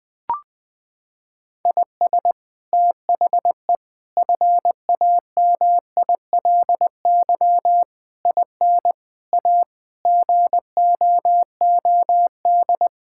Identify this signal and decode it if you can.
Morse code, heard as a keyed tone, about 20 wpm, 700 Hz. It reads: IS THE FAMILY IN A GOOD
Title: Sentences from Top 1000 Words - Review Part 5 - 20wpm